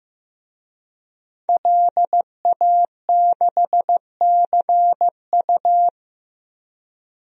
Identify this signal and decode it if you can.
Morse code, heard as a keyed tone, about 15 wpm, 700 Hz. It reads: LA6CU
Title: DXCC-100 Call Signs - 5 Characters - Repeated 3x - 15wpm